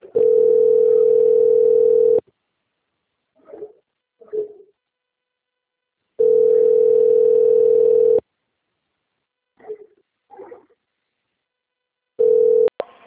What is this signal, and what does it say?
Ringback tone (North America), 440 + 480 Hz.